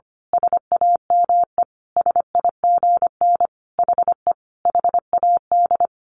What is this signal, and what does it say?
Morse code, heard as a keyed tone, about 25 wpm, 700 Hz. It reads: SAME SIGN HE HAD
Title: A Sound Of Thunder 25wpm